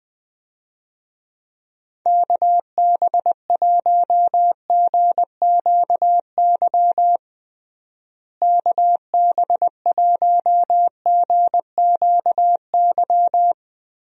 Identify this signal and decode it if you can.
Morse code, heard as a keyed tone, about 21 wpm, 700 Hz. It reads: KB1GQY KB1GQY
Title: USA Call Signs - 6 Characters - Repeated 3x - 20wpm